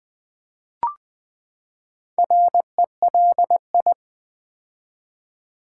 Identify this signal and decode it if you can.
Morse code, heard as a keyed tone, about 20 wpm, 700 Hz. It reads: RELI
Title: Build A Word - 4 Letters At A Time - 8-Letter Words - 20wpm